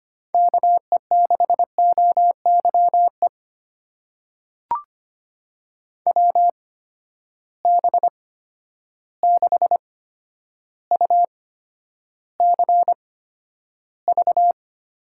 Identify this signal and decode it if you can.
Morse code, heard as a keyed tone, about 25 wpm, spaced as for 11 wpm, 700 Hz. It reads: KE6OYE WB6UCV
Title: USA Call Signs - 6 Characters - 8x Character Spacing - 25wpm